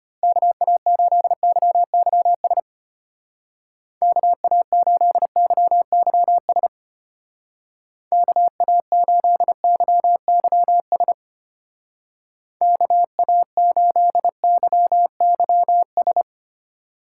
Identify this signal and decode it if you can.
Morse code, heard as a keyed tone, about 30 wpm, 700 Hz. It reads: KA8YYS KA8YYS KA8YYS KA8YYS
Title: Speed Racing - USA Call Signs 25wpm